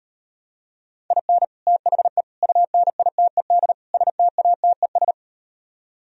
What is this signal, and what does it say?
Morse code, heard as a keyed tone, about 38 wpm, 700 Hz. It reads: IN THE UNITED STATES